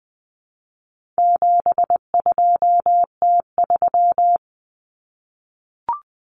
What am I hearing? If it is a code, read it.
Morse code, heard as a keyed tone, about 20 wpm, 700 Hz. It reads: 72T3
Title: Common Cut Numbers - 4-Digit 20wpm